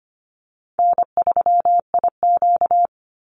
Morse code, heard as a keyed tone, about 25 wpm, 700 Hz. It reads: N3IQ
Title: Easy Call Signs - 25wpm